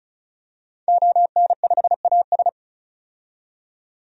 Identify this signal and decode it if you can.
Morse code, heard as a keyed tone, about 36 wpm, 700 Hz. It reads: ON5AS
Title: DXCC-100 Call Signs - 5 Characters - 35wpm